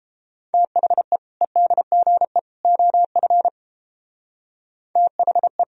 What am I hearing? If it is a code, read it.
Morse code, heard as a keyed tone, about 31 wpm, 700 Hz. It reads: THE EDGE OF THE